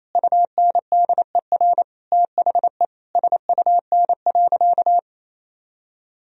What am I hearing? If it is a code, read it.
Morse code, heard as a keyed tone, about 28 wpm, 700 Hz. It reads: UNDER THE SUN.
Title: Sentences from Top 1000 Words - Review Part 1 - 28wpm